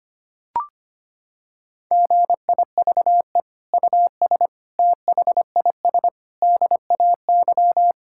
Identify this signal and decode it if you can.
Morse code, heard as a keyed tone, about 25 wpm, 700 Hz. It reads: GIVE US THIS DAY
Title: Sentences from Top 1600 Words 25wpm